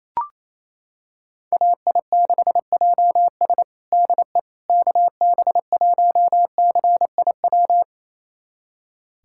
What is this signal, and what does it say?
Morse code, heard as a keyed tone, about 28 wpm, 700 Hz. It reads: AI6JS DE KB1CIW
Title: USA Callsign DE Callsign 28wpm